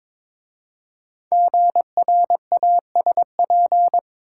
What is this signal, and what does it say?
Morse code, heard as a keyed tone, about 22 wpm, 700 Hz. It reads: GRASP
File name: 5 Letter Words - First 3 Letters Given - 22wpm